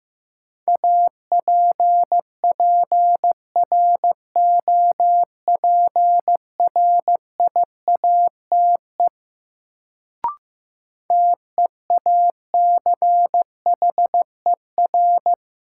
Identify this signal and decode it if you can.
Morse code, heard as a keyed tone, about 15 wpm, 700 Hz. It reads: APPROPRIATE TEACHER
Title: Sentences from Top 900 Words - New Word Drill - 15wpm